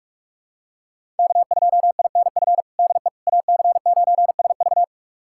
Morse code, heard as a keyed tone, about 46 wpm, 700 Hz. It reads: K2INF DE AC9SV
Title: USA Callsign DE Callsign - 5 Characters - 45wpm